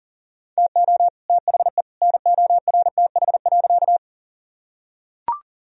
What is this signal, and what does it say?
Morse code, heard as a keyed tone, about 40 wpm, 700 Hz. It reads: TO THE NORTH.
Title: Sentences from Top 1000 Words - Review Part 1 - 40wpm